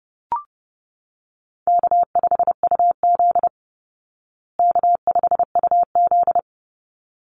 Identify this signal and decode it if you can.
Morse code, heard as a keyed tone, about 30 wpm, 700 Hz. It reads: K5UZ K5UZ
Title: Easy Call Signs Repeated 2x - 30wpm